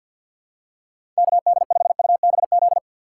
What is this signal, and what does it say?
Morse code, heard as a keyed tone, about 50 wpm, 700 Hz. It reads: KD5UBZ
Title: USA Call Signs - 6 Characters - 3x Character Spacing - 50wpm